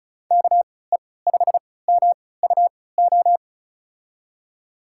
Morse code, heard as a keyed tone, about 36 wpm, spaced as for 20 wpm, 700 Hz. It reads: KE5MUO
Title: USA Call Signs - 6 Characters - 3x Character Spacing - 35wpm